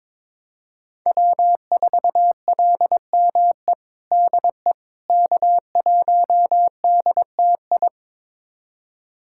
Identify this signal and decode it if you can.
Morse code, heard as a keyed tone, about 22 wpm, 700 Hz. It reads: W4LME DE K1DTI
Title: USA Callsign DE Callsign - 5 Characters - 22wpm